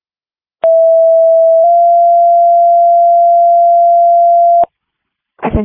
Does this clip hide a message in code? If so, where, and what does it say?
Yes: two-tone page from 0.6 to 4.6 s, heard as a one-second tone, then a three-second tone, about 660 then 690 Hz, with speech.